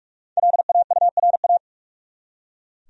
Morse code, heard as a keyed tone, about 46 wpm, 700 Hz. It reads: LAURA
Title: USA Common Names - 45wpm